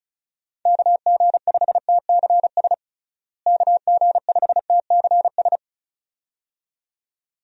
Morse code, heard as a keyed tone, about 36 wpm, 700 Hz. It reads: KG5TCS KG5TCS
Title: USA Call Signs - 6 Characters - Repeated 2x - 35wpm